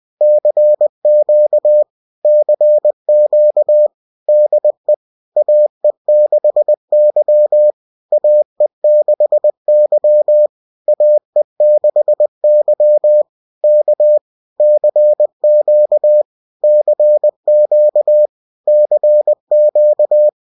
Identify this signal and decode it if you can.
Morse code, heard as a keyed tone, about 20 wpm, 600 Hz. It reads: CQ CQ DE AE6Y AE6Y AE6Y K CQ CQ CQ